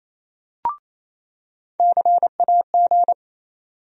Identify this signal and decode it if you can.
Morse code, heard as a keyed tone, about 28 wpm, 700 Hz. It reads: CAG
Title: USA Call Sign Postfixes - 3 Letters 28wpm